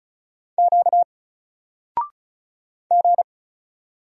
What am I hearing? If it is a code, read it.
Morse code, heard as a keyed tone, about 34 wpm, 700 Hz. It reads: Q G
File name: Single-Letter-Number-35wpm